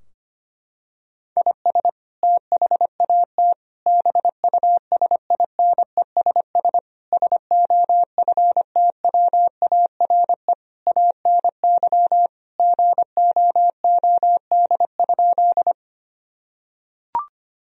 Morse code, heard as a keyed tone, about 25 wpm, 700 Hz. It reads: IS THAT BUSINESS SOFTWARE ANY GOOD?
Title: Sentences from Top 1000 Words - Review Part 3 - 25wpm